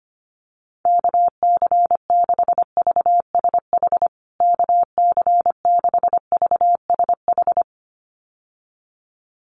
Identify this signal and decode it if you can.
Morse code, heard as a keyed tone, about 25 wpm, 700 Hz. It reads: KC6VSH KC6VSH
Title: USA Call Signs Repeated 2x 25wpm